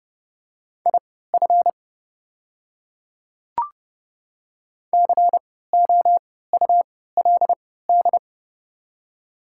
Morse code, heard as a keyed tone, about 32 wpm, spaced as for 17 wpm, 700 Hz. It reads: IF COULD
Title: 3x Character Spacing - Top 300 Words - 30wpm